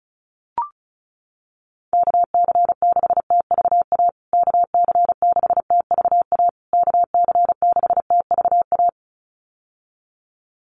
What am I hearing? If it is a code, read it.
Morse code, heard as a keyed tone, about 34 wpm, 700 Hz. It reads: KC6TVA KC6TVA KC6TVA
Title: USA Call Signs Repeated 3x 35wpm